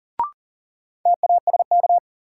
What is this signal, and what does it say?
Morse code, heard as a keyed tone, about 37 wpm, 700 Hz. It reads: TASK
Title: Sentences from Top 1100 Words - New Word Drill Alt - 40wpm